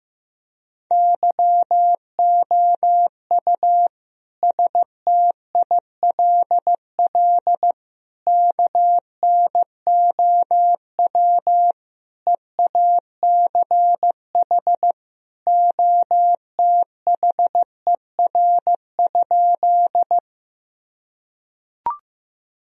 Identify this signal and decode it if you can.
Morse code, heard as a keyed tone, about 15 wpm, 700 Hz. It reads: YOU STILL KNOW EACH OTHER?
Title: Sentences from Top 1000 Words - Review Part 1 - 15wpm